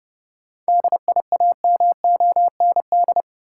Morse code, heard as a keyed tone, about 30 wpm, 700 Hz. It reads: DIAMOND
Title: Seven Letter Words - 30wpm